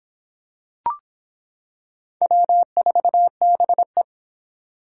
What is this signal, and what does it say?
Morse code, heard as a keyed tone, about 25 wpm, 700 Hz. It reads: W4BE